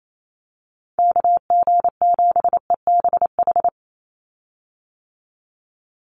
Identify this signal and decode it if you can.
Morse code, heard as a keyed tone, about 28 wpm, 700 Hz. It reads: KG7EBH
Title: USA Call Signs 28wpm